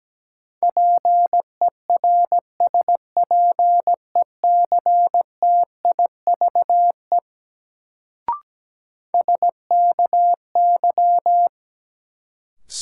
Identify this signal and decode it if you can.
Morse code, heard as a keyed tone, about 17 wpm, 700 Hz. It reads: PERSPECTIVE SKY
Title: Sentences from Top 1200 Words - New Word Drill Alt - 17wpm